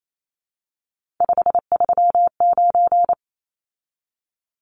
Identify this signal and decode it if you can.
Morse code, heard as a keyed tone, about 28 wpm, 700 Hz. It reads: 539
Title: Numbers - 3-Digit 28wpm